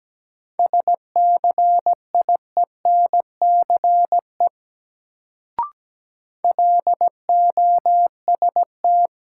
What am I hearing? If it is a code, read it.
Morse code, heard as a keyed tone, about 17 wpm, 700 Hz. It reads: SCIENCE LOST
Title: Sentences from Top 1400 Words - New Word Drill Alt - 17wpm